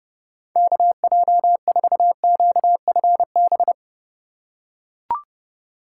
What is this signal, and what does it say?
Morse code, heard as a keyed tone, about 32 wpm, 700 Hz. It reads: KJ4QFB
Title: USA Call Signs - 6 Characters - 4x Character Spacing - 30wpm